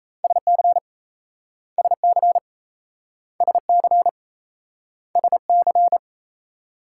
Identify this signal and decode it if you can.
Morse code, heard as a keyed tone, about 38 wpm, 700 Hz. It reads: SC SC SC SC